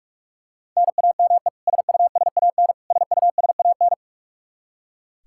Morse code, heard as a keyed tone, about 41 wpm, 700 Hz. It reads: NAME SUSAN SUSAN
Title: Practice QSO - Exchange 1 - Beginning - 45wpm